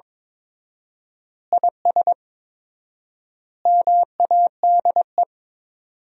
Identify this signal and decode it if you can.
Morse code, heard as a keyed tone, about 22 wpm, 700 Hz. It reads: IS MADE
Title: 4x Word-Spacing - Sentences from Top 500 Words - 22wpm